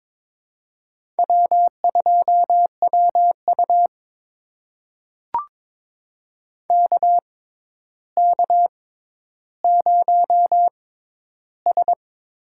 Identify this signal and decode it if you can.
Morse code, heard as a keyed tone, about 22 wpm, spaced as for 11 wpm, 700 Hz. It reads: W2WU KK0S
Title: USA Call Signs - 4 Characters - 6x Character Spacing - 22wpm